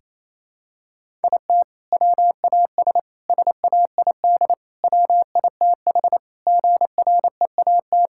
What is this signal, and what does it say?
Morse code, heard as a keyed tone, about 28 wpm, 700 Hz. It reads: IT WAS SAID WITH GREAT
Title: Sentences from Top 1600 Words 28wpm